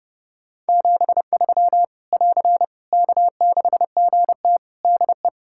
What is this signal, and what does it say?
Morse code, heard as a keyed tone, about 30 wpm, 700 Hz. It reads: 73 <AR> K6GT DE
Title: Practice QSO - Exchange 4 - Ending - 30wpm